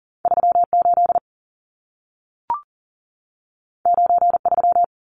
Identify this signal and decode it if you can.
Morse code, heard as a keyed tone, about 40 wpm, 700 Hz. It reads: 38 93
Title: Numbers - 2-Digit 40wpm